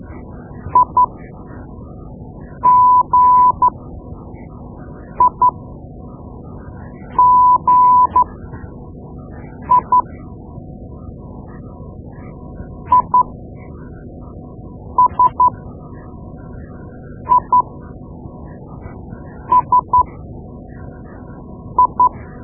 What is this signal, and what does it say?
Morse code, heard as a keyed tone, about 12 wpm, spaced as for 4 wpm, 1000 Hz. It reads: IGIGI ISISI
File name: Morse Lessons / Morse Lessons / Tape-2B